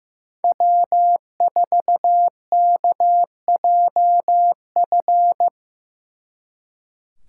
Morse code, heard as a keyed tone, about 18 wpm, 700 Hz. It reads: W4KJF
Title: Speed Racing - USA Call Signs 15wpm